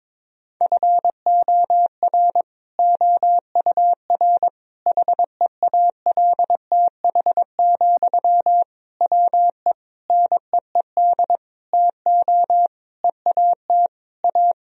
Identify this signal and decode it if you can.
Morse code, heard as a keyed tone, about 22 wpm, 700 Hz. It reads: FOR OUR HEALTH, WE NEED TO EAT A
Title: Sentences from Top 1000 Words - Review Part 4 - 22wpm